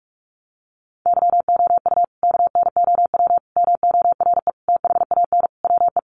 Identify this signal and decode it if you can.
Morse code, heard as a keyed tone, about 44 wpm, 700 Hz. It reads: YOU KNOW MORE THAN WE
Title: Sentences from Top 100 Words 45wpm